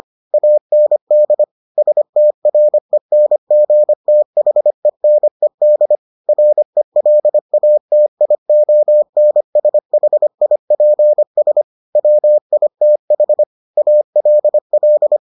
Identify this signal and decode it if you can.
Morse code, heard as a keyed tone, about 25 wpm, 600 Hz. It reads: AND STRENGTHENED RELATIONSHIPS WITH ALL